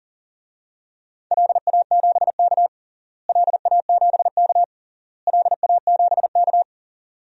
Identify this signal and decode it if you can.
Morse code, heard as a keyed tone, about 37 wpm, 700 Hz. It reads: LA7K LA7K LA7K
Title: DXCC-100 Call Signs - 4 Characters - Repeated 3x - 40wpm